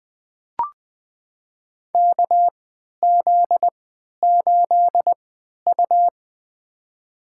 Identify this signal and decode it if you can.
Morse code, heard as a keyed tone, about 20 wpm, spaced as for 11 wpm, 700 Hz. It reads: KZ8U
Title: USA Call Signs - 4 Characters - 3x Character Spacing - 20wpm